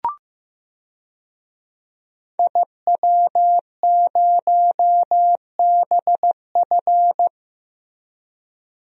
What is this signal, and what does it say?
Morse code, heard as a keyed tone, about 15 wpm, 700 Hz. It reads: IW0BF
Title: DXCC-100 Call Signs - 5 Characters - Repeated 2x - 15wpm